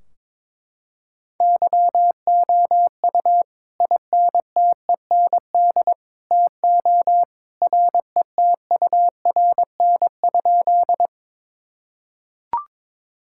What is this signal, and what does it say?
Morse code, heard as a keyed tone, about 22 wpm, 700 Hz. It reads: YOU INTEND TO RETURN?
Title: Sentences from Top 1600 Words 22wpm